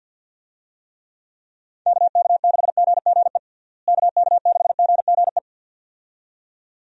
Morse code, heard as a keyed tone, about 48 wpm, 700 Hz. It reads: KK6GGE KK6GGE
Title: USA Call Signs - 6 Characters - Repeated 2x - 50wpm